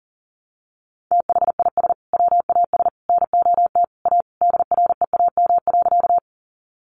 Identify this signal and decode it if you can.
Morse code, heard as a keyed tone, about 37 wpm, 700 Hz. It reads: THIS WAS NOT A DREAM.
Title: Sentences from Top 1500 Words 40wpm